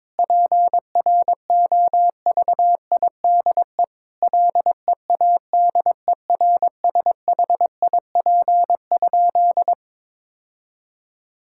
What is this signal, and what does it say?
Morse code, heard as a keyed tone, about 22 wpm, 700 Hz. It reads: PROVIDE LEADERSHIP?
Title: Sentences from Top 1600 Words 22wpm